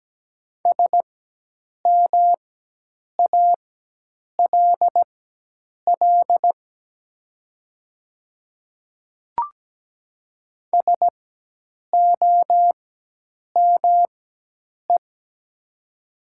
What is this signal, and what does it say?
Morse code, heard as a keyed tone, about 17 wpm, spaced as for 8 wpm, 700 Hz. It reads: SMALL SOME
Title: 4x Character Spacing - Top 200 Words - 17wpm